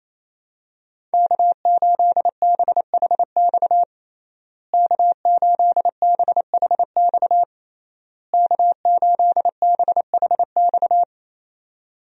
Morse code, heard as a keyed tone, about 28 wpm, 700 Hz. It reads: K8BHX K8BHX K8BHX
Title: USA Call Signs - 5 Characters - Repeated 3x - 28wpm